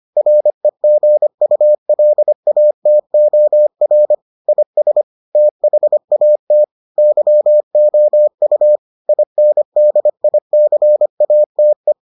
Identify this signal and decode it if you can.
Morse code, heard as a keyed tone, about 25 wpm, 600 Hz. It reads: REGULATOR IS THAT YOU INDICATE